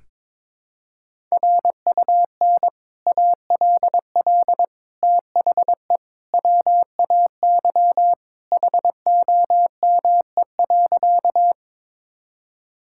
Morse code, heard as a keyed tone, about 22 wpm, 700 Hz. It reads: RUN ALL THE WAY HOME.
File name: practice / 22wpm / Sentences with 3-Word Phrases - 22wpm